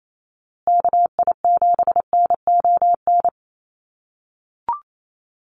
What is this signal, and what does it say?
Morse code, heard as a keyed tone, about 28 wpm, 700 Hz. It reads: KI7NON
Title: USA Call Signs Repeated 3x 28wpm